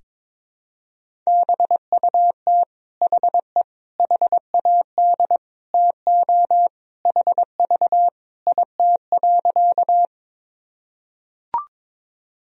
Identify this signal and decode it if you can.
Morse code, heard as a keyed tone, about 22 wpm, 700 Hz. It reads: BUT HE HAD TO HV IT.
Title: Sentences from Top 100 Words and Abbreviations 22wpm